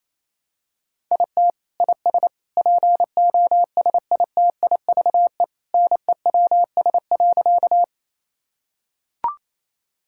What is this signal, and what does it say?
Morse code, heard as a keyed tone, about 27 wpm, 700 Hz. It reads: IT IS POSITIVE NEWS.
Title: Sentences from Top 1000 Words - Review Part 5 - 28wpm